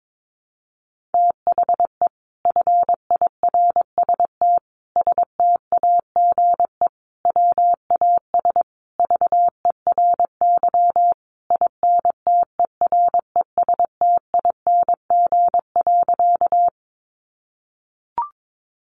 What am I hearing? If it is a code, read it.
Morse code, heard as a keyed tone, about 22 wpm, 700 Hz. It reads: THE FIRST STAGE WAS VERY INTERESTING.
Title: Sentences from Top 1200 Words 22wpm